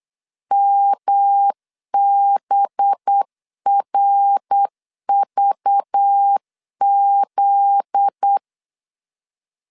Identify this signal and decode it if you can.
Morse code, heard as a keyed tone, about 8 wpm, 800 Hz. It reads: MBRVZ
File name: LEZIONI / 7 GF9Z / 7 GF9Z 8wpm 3213